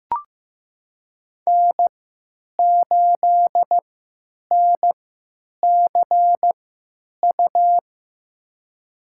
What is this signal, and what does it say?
Morse code, heard as a keyed tone, about 15 wpm, spaced as for 8 wpm, 700 Hz. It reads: N8NCU
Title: USA Call Signs - 5 Characters - 3x Character Spacing - 15wpm